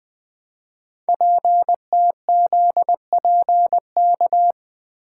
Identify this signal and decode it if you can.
Morse code, heard as a keyed tone, about 20 wpm, 700 Hz. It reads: PTZPK
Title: Code Groups - 5 Letters - 20wpm